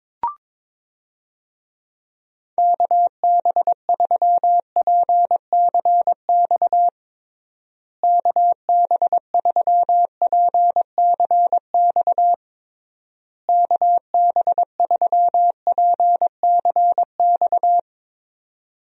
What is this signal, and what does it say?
Morse code, heard as a keyed tone, about 22 wpm, 700 Hz. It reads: KB3PCX KB3PCX KB3PCX
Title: USA Call Signs - 6 Characters - Repeated 3x - 22wpm